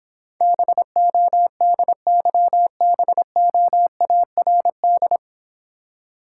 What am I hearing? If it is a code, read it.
Morse code, heard as a keyed tone, about 26 wpm, 700 Hz. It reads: BODYBOARD